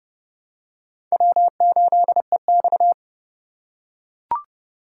Morse code, heard as a keyed tone, about 32 wpm, 700 Hz. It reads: W8EX
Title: USA Call Signs - 4 Characters - 4x Character Spacing - 30wpm